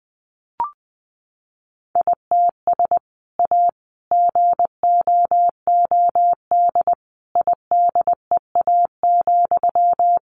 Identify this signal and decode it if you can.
Morse code, heard as a keyed tone, about 20 wpm, 700 Hz. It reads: ITS A GOOD IDEA,
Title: Sentences from Top 600 Words 20wpm